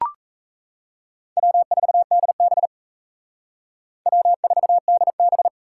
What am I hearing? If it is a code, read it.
Morse code, heard as a keyed tone, about 35 wpm, 700 Hz. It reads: W4DB W4DB
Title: Speed Racing - USA 1x2 Call Signs 28wpm